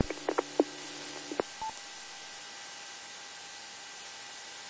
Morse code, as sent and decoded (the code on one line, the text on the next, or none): none